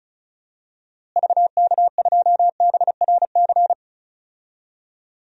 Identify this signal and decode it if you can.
Morse code, heard as a keyed tone, about 36 wpm, 700 Hz. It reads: VK2BRC
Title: DXCC-100 Call Signs - 6 Characters - Repeated 3x - 35wpm